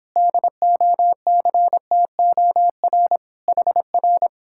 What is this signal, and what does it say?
Morse code, heard as a keyed tone, about 31 wpm, 700 Hz. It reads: DOCTOR HR